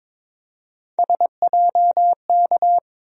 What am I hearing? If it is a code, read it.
Morse code, heard as a keyed tone, about 22 wpm, 700 Hz. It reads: SJK
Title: USA Call Sign Postfixes - 3 Letters 22wpm